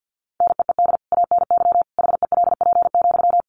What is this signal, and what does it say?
Morse code, heard as a keyed tone, about 50 wpm, 700 Hz. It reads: NEED ANY HELP,
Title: Sentences from Top 1200 Words 50wpm